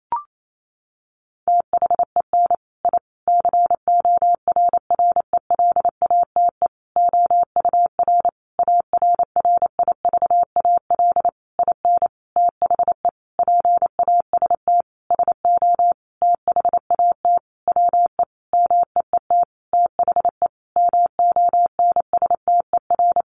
Morse code, heard as a keyed tone, about 28 wpm, 700 Hz. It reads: THEN I CORRELATE OUR ARRIVAL IN THE PAST SO THAT WE MEET THE MONSTER
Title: A Sound Of Thunder 28wpm